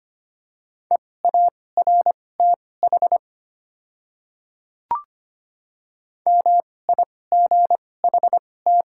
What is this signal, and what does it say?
Morse code, heard as a keyed tone, about 25 wpm, spaced as for 18 wpm, 700 Hz. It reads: EARTH MIGHT
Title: 2x Character Spacing - Top 300 Words - 25wpm